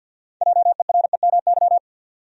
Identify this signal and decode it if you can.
Morse code, heard as a keyed tone, about 53 wpm, 700 Hz. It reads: JEREMY
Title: USA Common Names - 50wpm